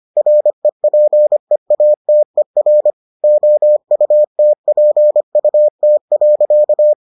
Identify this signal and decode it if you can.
Morse code, heard as a keyed tone, about 25 wpm, 600 Hz. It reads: REPEATER OUTPUT.